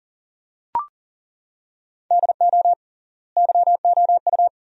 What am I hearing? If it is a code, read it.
Morse code, heard as a keyed tone, about 40 wpm, 700 Hz. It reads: DO YOU
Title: 3x Word-Spacing - Sentences from Top 300 Words - 40wpm